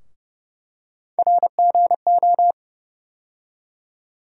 Morse code, heard as a keyed tone, about 30 wpm, 700 Hz. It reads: RGO
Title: USA Call Sign Postfixes - 3 Letters 30wpm